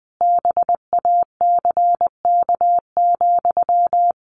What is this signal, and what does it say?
Morse code, heard as a keyed tone, about 20 wpm, 700 Hz. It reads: BACK,
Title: A Sound Of Thunder 20wpm